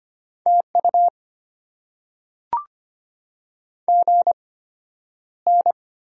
Morse code, heard as a keyed tone, about 25 wpm, spaced as for 11 wpm, 700 Hz. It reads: TU GN
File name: Top 90 QSO Elements - 8x Character Spacing - 25wpm